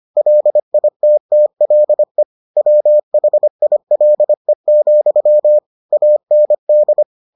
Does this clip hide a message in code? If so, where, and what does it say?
Yes: Morse code from 0.2 to 7.0 s, heard as a keyed tone, about 25 wpm, 600 Hz. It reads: LITTLE WHILE, AND